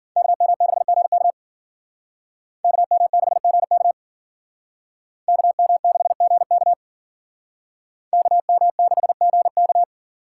Morse code, heard as a keyed tone, about 51 wpm, 700 Hz. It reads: KM6GK KM6GK KM6GK KM6GK